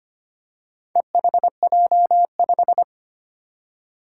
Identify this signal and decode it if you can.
Morse code, heard as a keyed tone, about 25 wpm, 700 Hz. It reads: EHJ5
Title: Four Challenging Letter-Number Combinations 25wpm